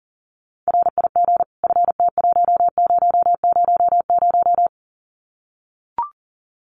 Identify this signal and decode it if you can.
Morse code, heard as a keyed tone, about 40 wpm, 700 Hz. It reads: RIG FT1000
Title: Common QSO Phrases 40wpm